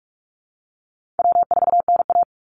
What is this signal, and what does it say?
Morse code, heard as a keyed tone, about 45 wpm, 700 Hz. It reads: W4NA
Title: Easy Call Signs Repeated 2x - 45wpm